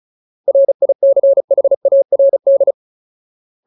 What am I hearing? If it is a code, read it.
Morse code, heard as a keyed tone, about 34 wpm, 550 Hz. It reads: RICHARD
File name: Ultimate Word List / 35WPM Ultimate